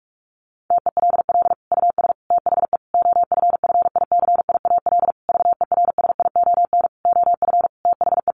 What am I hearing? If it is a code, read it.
Morse code, heard as a keyed tone, about 44 wpm, 700 Hz. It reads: TELL US THE OFFICIAL VERSION OF THE
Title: Sentences from Top 1400 Words 45wpm